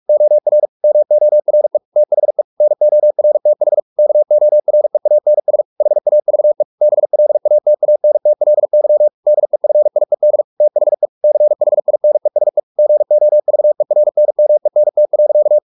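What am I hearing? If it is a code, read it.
Morse code, heard as a keyed tone, about 44 wpm, 600 Hz. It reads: OR MORE THE NORTH KOREANS HAVE BLATANTLY DEFIED THE CHINESE GOVERNMENT.
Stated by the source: Morse practice at 45 wpm, the novel Never by Ken Follett in English, from CW med Gnister.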